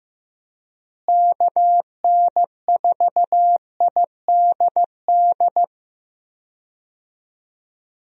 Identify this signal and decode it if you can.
Morse code, heard as a keyed tone, about 15 wpm, 700 Hz. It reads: KN4IDD
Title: USA Call Signs - 6 Characters - Repeated 3x - 15wpm